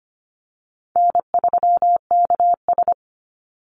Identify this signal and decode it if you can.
Morse code, heard as a keyed tone, about 25 wpm, 700 Hz. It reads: N3KS
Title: Easy Call Signs - 25wpm